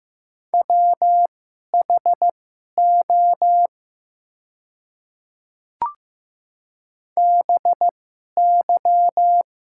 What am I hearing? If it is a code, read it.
Morse code, heard as a keyed tone, about 15 wpm, spaced as for 11 wpm, 700 Hz. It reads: WHO BY
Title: 2x Character Spacing - Top 100 Words - 15wpm